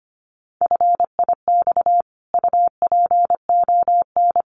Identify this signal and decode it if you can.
Morse code, heard as a keyed tone, about 25 wpm, 700 Hz. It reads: FIX UPON
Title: Sentences from Top 1400 Words 25wpm